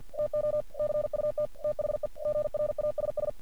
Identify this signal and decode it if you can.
Morse code, heard as a keyed tone, about 50 wpm, 600 Hz. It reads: TO CUT THE GRASS
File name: archived / CRSnet